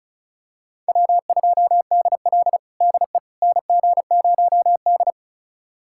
Morse code, heard as a keyed tone, about 34 wpm, 700 Hz. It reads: W2DL DE NG0D
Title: USA Callsign DE Callsign - 4 Characters - 35wpm